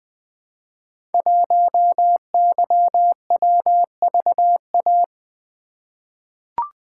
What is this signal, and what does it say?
Morse code, heard as a keyed tone, about 20 wpm, 700 Hz. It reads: 1YWVA
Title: Code Groups - 5 Letter:Numbers - 20wpm